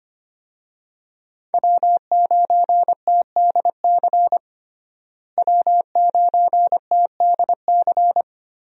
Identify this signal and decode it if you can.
Morse code, heard as a keyed tone, about 25 wpm, 700 Hz. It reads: W9TDC W9TDC
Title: USA Call Signs - 5 Characters - Repeated 2x - 25wpm